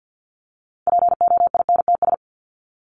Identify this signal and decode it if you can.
Morse code, heard as a keyed tone, about 50 wpm, 700 Hz. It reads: POINTS
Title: ES:S Suffix - Top 100 Words with Base Words 50wpm